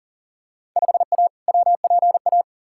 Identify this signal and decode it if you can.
Morse code, heard as a keyed tone, about 40 wpm, 700 Hz. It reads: 5A WPA
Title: Field Day Exchanges Encore - 40wpm